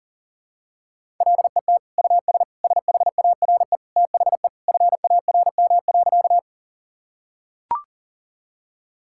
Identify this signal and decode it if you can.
Morse code, heard as a keyed tone, about 40 wpm, 700 Hz. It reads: LET US SHARE THE FARM.
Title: Sentences from Top 1000 Words - Review Part 6 - 40wpm